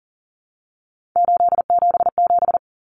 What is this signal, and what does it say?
Morse code, heard as a keyed tone, about 40 wpm, 700 Hz. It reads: 877
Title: Common Cut Numbers - 3-Digit 40wpm